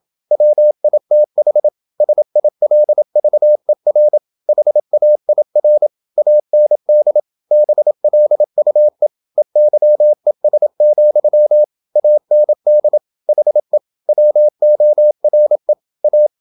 Morse code, heard as a keyed tone, about 27 wpm, 600 Hz. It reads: WITH SILVER HAIR AND BLUE EYES, AND HE WORE A
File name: Never 0027wpm